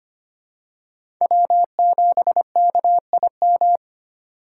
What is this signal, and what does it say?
Morse code, heard as a keyed tone, about 25 wpm, 700 Hz. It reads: W7KIM
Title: USA Call Signs - 5 Characters - 6x Character Spacing - 25wpm